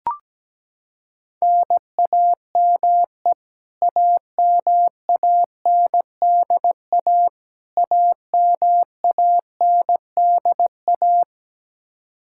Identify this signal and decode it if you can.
Morse code, heard as a keyed tone, about 17 wpm, 700 Hz. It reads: NAME AMANDA AMANDA
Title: K1USN SST QSO Practice 17wpm